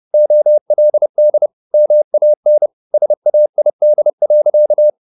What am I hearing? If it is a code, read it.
Morse code, heard as a keyed tone, about 30 wpm, 600 Hz. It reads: OLD MAN SAID.
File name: the old man and the sea / the old man and the sea 1 30wpm